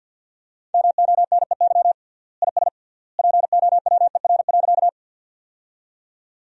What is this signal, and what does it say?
Morse code, heard as a keyed tone, about 48 wpm, 700 Hz. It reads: MONEY IS POWER.
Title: 3x Word-Spacing - Sentences from Top 500 Words - 50wpm